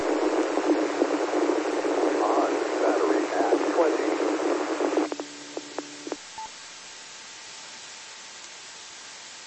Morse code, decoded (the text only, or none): none